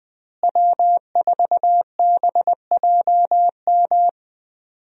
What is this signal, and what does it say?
Morse code, heard as a keyed tone, about 20 wpm, 700 Hz. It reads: W4BJM
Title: USA Call Signs - 5 Characters - Repeated 3x - 20wpm